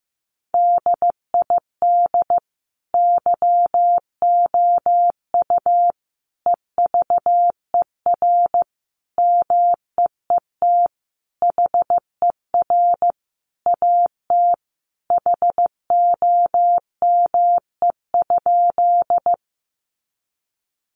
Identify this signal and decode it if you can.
Morse code, heard as a keyed tone, about 15 wpm, 700 Hz. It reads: DID YOU EVER MEET HER AT HOME?
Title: Sentences from Top 1100 Words 15wpm